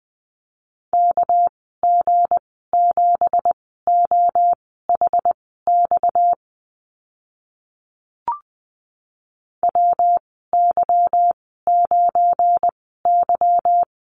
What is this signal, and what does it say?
Morse code, heard as a keyed tone, about 20 wpm, spaced as for 15 wpm, 700 Hz. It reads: KG7OHX WY9Y
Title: 231 Lesson 37g - Letter x - Call Signs 20wpm